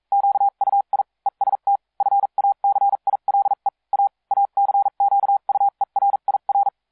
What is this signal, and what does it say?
Morse code, heard as a keyed tone, about 42 wpm, spaced as for 36 wpm, 800 Hz. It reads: QUI EST FACILE A ACQUERIR